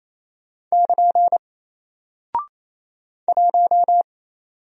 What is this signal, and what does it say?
Morse code, heard as a keyed tone, about 27 wpm, 700 Hz. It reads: <KN> 1
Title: Single Character (Letter:Number:Punctuation:Prosign) Alt 28wpm